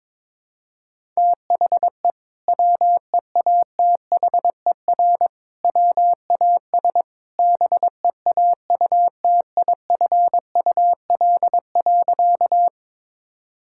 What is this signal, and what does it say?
Morse code, heard as a keyed tone, about 21 wpm, 700 Hz. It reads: THE WEATHER WAS BEAUTIFUL.
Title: Sentences from Top 1000 Words - Review Part 5 - 22wpm